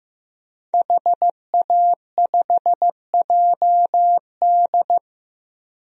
Morse code, heard as a keyed tone, about 15 wpm, 700 Hz. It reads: HA5JD
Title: DXCC-100 Call Signs - 5 Characters - 15wpm